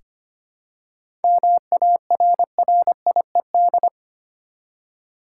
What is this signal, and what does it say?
Morse code, heard as a keyed tone, about 25 wpm, 700 Hz. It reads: MARRIED
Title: Seven Letter Words - 25wpm